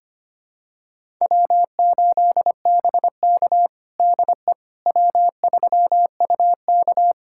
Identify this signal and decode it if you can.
Morse code, heard as a keyed tone, about 25 wpm, 700 Hz. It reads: W8BK DE W3UK
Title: USA Callsign DE Callsign - 4 Characters - 25wpm